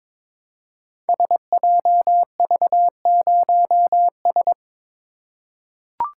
Morse code, heard as a keyed tone, about 22 wpm, 700 Hz. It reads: SJV0S
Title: Code Groups - 5 Letter:Numbers - 22wpm